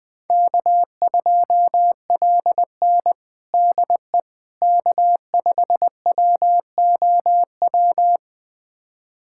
Morse code, heard as a keyed tone, about 21 wpm, 700 Hz. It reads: K2LN DE K5WOW
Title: USA Callsign DE Callsign 20wpm